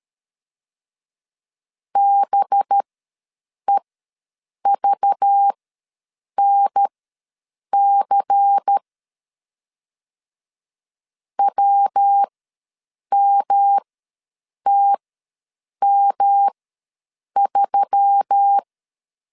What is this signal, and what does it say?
Morse code, heard as a keyed tone, about 13 wpm, spaced as for 7 wpm, 800 Hz. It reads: BEVNC WMTM3
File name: LEZIONI / 7 GF9Z / 7 GF ws 12wpm 3140